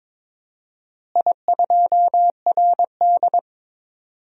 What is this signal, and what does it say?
Morse code, heard as a keyed tone, about 22 wpm, 700 Hz. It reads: I2RD
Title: DXCC-100 Call Signs - 4 Characters - Repeated 3x - 22wpm